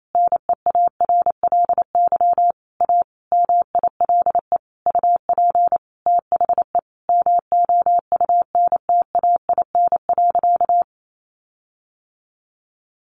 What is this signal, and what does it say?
Morse code, heard as a keyed tone, about 28 wpm, 700 Hz. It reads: NEARLY A MILE UP THE MOUNTAIN.
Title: Sentences from Top 1300 Words 28wpm